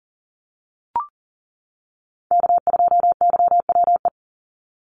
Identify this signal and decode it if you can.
Morse code, heard as a keyed tone, about 41 wpm, 700 Hz. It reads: K2YWE
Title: Call Signs - 40wpm